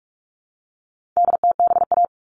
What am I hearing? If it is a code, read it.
Morse code, heard as a keyed tone, about 44 wpm, 700 Hz. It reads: DTBA
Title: All Cut Numbers - 4-Digit 45wpm